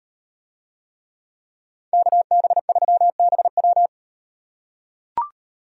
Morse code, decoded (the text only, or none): KB3BW